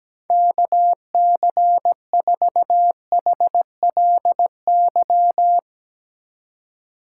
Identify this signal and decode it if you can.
Morse code, heard as a keyed tone, about 19 wpm, 700 Hz. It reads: KC4HLY
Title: Speed Racing - USA 2x3 Call Signs 17wpm